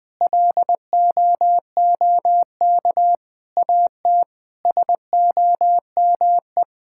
Morse code, heard as a keyed tone, about 20 wpm, 700 Hz. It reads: LOOK AT SOME
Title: Sentences with 8-Word Phrases - 20wpm